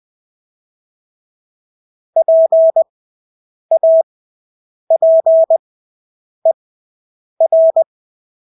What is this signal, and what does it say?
Morse code, heard as a keyed tone, about 20 wpm, spaced as for 8 wpm, 650 Hz. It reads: PAPER
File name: Engelske ord 20/8 wpm